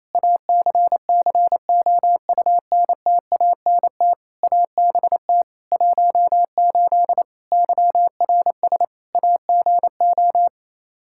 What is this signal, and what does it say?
Morse code, heard as a keyed tone, about 28 wpm, 700 Hz. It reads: ACCOUNTANT ABT 18 YRS AGO
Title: K1USN SST QSO Practice 28wpm